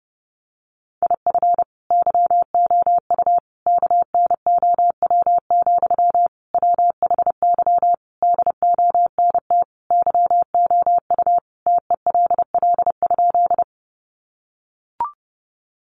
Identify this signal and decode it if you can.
Morse code, heard as a keyed tone, about 30 wpm, 700 Hz. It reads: IF YOU KNOW, WHY DONT YOU TELL?
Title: Sentences from Top 300 Words 30wpm